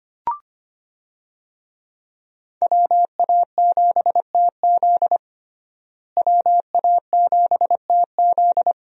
Morse code, heard as a keyed tone, about 25 wpm, 700 Hz. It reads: WA7TZ WA7TZ
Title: USA Call Signs - 5 Characters - Repeated 2x - 25wpm